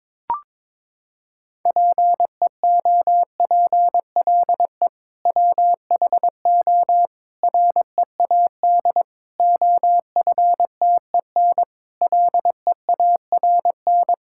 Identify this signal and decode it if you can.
Morse code, heard as a keyed tone, about 21 wpm, 700 Hz. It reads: PEOPLE WHO READ OFTEN LEARN
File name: practice / 22wpm / Sentences with 2-Word Phrases - 22wpm